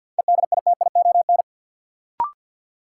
Morse code, heard as a keyed tone, about 52 wpm, 700 Hz. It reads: EDITION
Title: Sets of 5 Words - New Word Drill - 50wpm